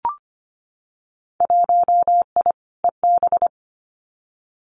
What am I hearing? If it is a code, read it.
Morse code, heard as a keyed tone, about 25 wpm, 700 Hz. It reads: 1I EB
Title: Winter Field Day Exchanges Encore 25wpm